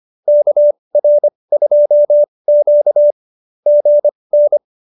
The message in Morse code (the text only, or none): KR2Q GN